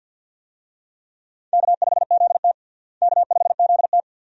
Morse code, heard as a keyed tone, about 45 wpm, 700 Hz. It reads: K5ZT K5ZT
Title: USA Call Signs - 4 Characters - Repeated 2x - 50wpm